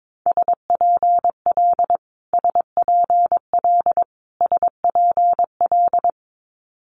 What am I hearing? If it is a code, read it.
Morse code, heard as a keyed tone, about 22 wpm, 700 Hz. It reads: SPL SPL SPL
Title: Multi-letter Phonemes Repeated 3x 22wpm